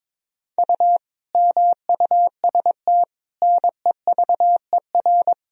Morse code, heard as a keyed tone, about 22 wpm, 700 Hz. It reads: U MUST NEVER
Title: Sentences from Top 300 Words and Abbreviations 22wpm